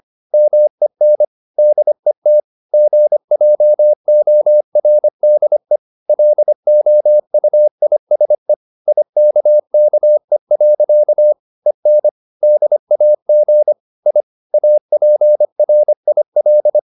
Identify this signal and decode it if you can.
Morse code, heard as a keyed tone, about 25 wpm, 600 Hz. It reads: MEN DET GJORDE LOUISE IKKE. EN DAG I APRIL
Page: Ferie 25 wpm